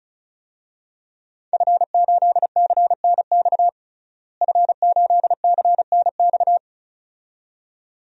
Morse code, heard as a keyed tone, about 34 wpm, 700 Hz. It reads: F8CNX F8CNX
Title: DXCC-100 Call Signs - 5 Characters - Repeated 2x - 35wpm